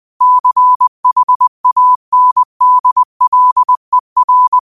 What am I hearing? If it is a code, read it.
Morse code, heard as a keyed tone, about 20 wpm, 1000 Hz. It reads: CHANDLER
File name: qotdfiles / 20wpm